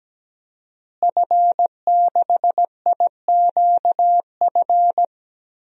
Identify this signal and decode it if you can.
Morse code, heard as a keyed tone, about 17 wpm, 700 Hz. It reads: F6IQF
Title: DXCC-100 Call Signs - 5 Characters - Repeated 2x - 17wpm